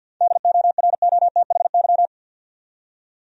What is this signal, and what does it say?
Morse code, heard as a keyed tone, about 50 wpm, 700 Hz. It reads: DOROTHY
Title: USA Common Names - 50wpm